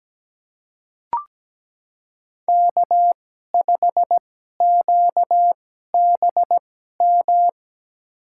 Morse code, heard as a keyed tone, about 17 wpm, spaced as for 12 wpm, 700 Hz. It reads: K5QBM
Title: USA Call Signs - 5 Characters - 2x Character Spacing - 17wpm